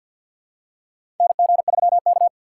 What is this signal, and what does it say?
Morse code, heard as a keyed tone, about 50 wpm, 700 Hz. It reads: NG3K
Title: USA Call Signs - 4 Characters - 6x Character Spacing - 50wpm